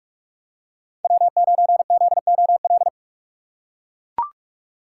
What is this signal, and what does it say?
Morse code, heard as a keyed tone, about 43 wpm, 700 Hz. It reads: W9ZOL
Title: USA Call Signs - 5 Characters - 2x Character Spacing - 45wpm